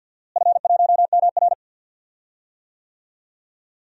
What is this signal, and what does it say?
Morse code, heard as a keyed tone, about 49 wpm, 700 Hz. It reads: F1MR
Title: DXCC-100 Call Signs - 4 Characters - Repeated 2x - 50wpm